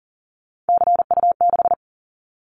Morse code, heard as a keyed tone, about 40 wpm, 700 Hz. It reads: CU6
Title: Callsign Prefixes for Top 100 DXCC - 40wpm